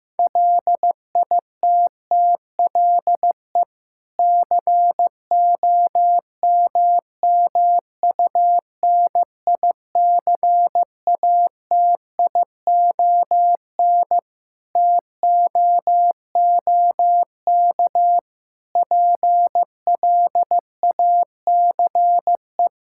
Morse code, heard as a keyed tone, about 15 wpm, 700 Hz. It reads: LITTLE COMMUNICATION TOOK PLACE
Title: Sentences from Top 1000 Words - Review Part 6 - 15wpm